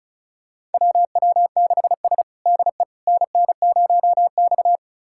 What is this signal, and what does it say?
Morse code, heard as a keyed tone, about 36 wpm, 700 Hz. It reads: WW6S DE NN0X
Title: USA Callsign DE Callsign - 4 Characters - 35wpm